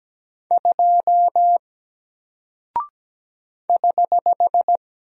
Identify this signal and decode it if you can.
Morse code, heard as a keyed tone, about 17 wpm, 700 Hz. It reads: 2 <HH>
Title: Single Character (Letter:Number:Punctuation:Prosign) 17wpm